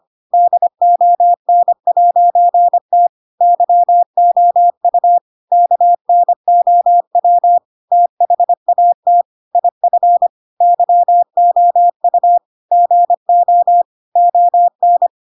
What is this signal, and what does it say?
Morse code, heard as a keyed tone, about 25 wpm, 700 Hz. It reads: DON'T YOU KNOW THAT IF YOU GO ON
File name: Pinocchio / Pinocchio ch4 25 25 ik1hge